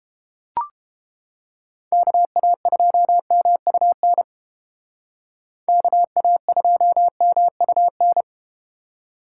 Morse code, heard as a keyed tone, about 30 wpm, 700 Hz. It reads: KA2MUN KA2MUN